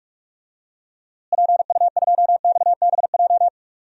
Morse code, heard as a keyed tone, about 46 wpm, 700 Hz. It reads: PU2XBJ
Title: DXCC-100 Call Signs - 6 Characters - Repeated 2x - 45wpm